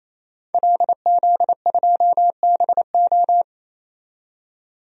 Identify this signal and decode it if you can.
Morse code, heard as a keyed tone, about 27 wpm, 700 Hz. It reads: LZ2BO
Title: DXCC-100 Call Signs - 5 Characters - 28wpm